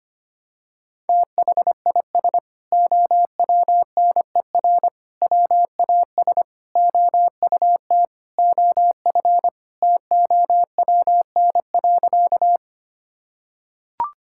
Morse code, heard as a keyed tone, about 25 wpm, 700 Hz. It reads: THIS OWNER WAS OUT OF TOWN.
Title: Sentences from Top 1600 Words 25wpm